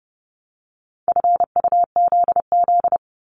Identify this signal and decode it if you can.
Morse code, heard as a keyed tone, about 30 wpm, 700 Hz. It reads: FUZZ
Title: Multi-letter Phonemes 30wpm